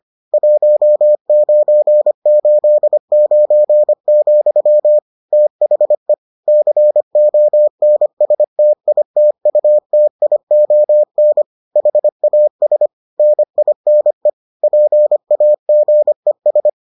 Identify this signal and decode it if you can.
Morse code, heard as a keyed tone, about 25 wpm, 600 Hz. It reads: 1989, THE CONSTITUTION HAS NINE PAGES